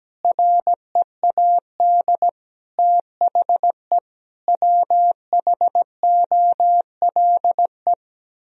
Morse code, heard as a keyed tone, about 17 wpm, 700 Hz. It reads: READ THE WHOLE
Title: Sentences from Top 1000 Words - Review Part 1 - 17wpm